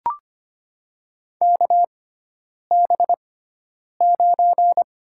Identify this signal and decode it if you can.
Morse code, heard as a keyed tone, about 25 wpm, spaced as for 9 wpm, 700 Hz. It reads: KB9
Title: USA Call Signs - 6 Characters - 6x Character Spacing - 25wpm